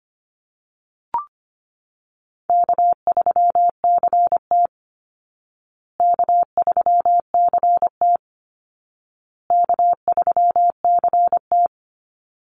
Morse code, heard as a keyed tone, about 25 wpm, 700 Hz. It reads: K3CT K3CT K3CT
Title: Call Signs Repeated 2x - 25wpm